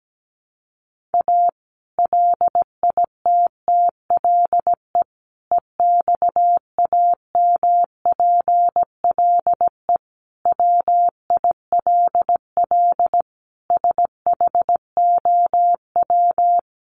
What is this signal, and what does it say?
Morse code, heard as a keyed tone, about 17 wpm, 700 Hz. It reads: A LITTLE EXAMPLE WILL SHOW
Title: Sentences from Top 1500 Words 17wpm